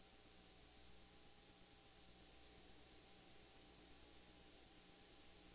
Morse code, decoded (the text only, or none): none